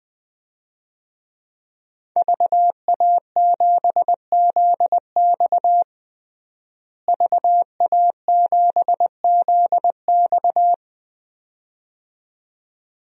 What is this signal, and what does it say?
Morse code, heard as a keyed tone, about 21 wpm, 700 Hz. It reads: VA7ZX VA7ZX
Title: DXCC-100 Call Signs - 5 Characters - Repeated 2x - 20wpm